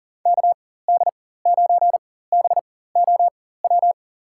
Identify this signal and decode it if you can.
Morse code, heard as a keyed tone, about 41 wpm, spaced as for 19 wpm, 700 Hz. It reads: KD9BOW
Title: USA Call Signs - 6 Characters - 4x Character Spacing - 40wpm